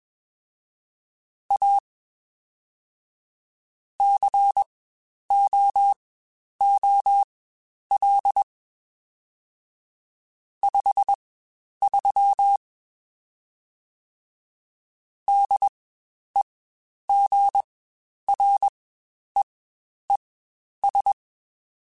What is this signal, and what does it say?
Morse code, heard as a keyed tone, about 21 wpm, spaced as for 10 wpm, 800 Hz. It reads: A COOL 53 DEGREES
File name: Spaced Characters / Spaced Characters-mp3s / Weather5 1 sp @20